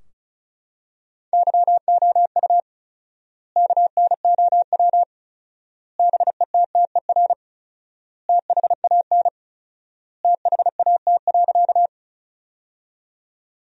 Morse code, heard as a keyed tone, about 36 wpm, 700 Hz. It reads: YOU KNOW BETTER THAN THAT.
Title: 4x Word-Spacing - Sentences from Top 400 Words - 35wpm